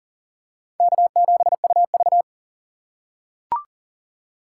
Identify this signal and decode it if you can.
Morse code, heard as a keyed tone, about 40 wpm, 700 Hz. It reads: K7UV
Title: USA Call Signs - 4 Characters - 4x Character Spacing - 40wpm